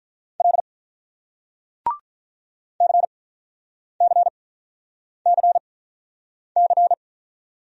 Morse code, heard as a keyed tone, about 47 wpm, spaced as for 9 wpm, 700 Hz. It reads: L CCCC